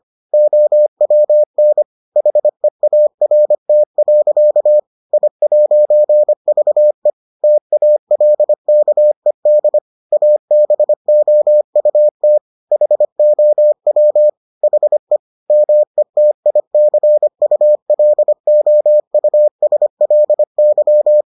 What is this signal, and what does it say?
Morse code, heard as a keyed tone, about 25 wpm, 600 Hz. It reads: OWN HEART. I'VE TALKED ABOUT HOW HE METICULOUSLY